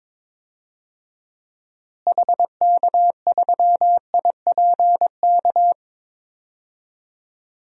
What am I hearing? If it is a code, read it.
Morse code, heard as a keyed tone, about 22 wpm, 700 Hz. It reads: HK3IPK
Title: DXCC-100 Call Signs - 6 Characters - Repeated 3x - 22wpm